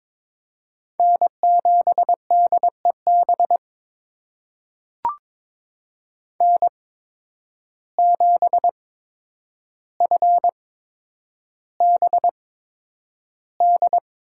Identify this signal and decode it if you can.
Morse code, heard as a keyed tone, about 22 wpm, spaced as for 9 wpm, 700 Hz. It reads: N7DEB N7FBD
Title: USA Call Signs - 5 Characters - 8x Character Spacing - 22wpm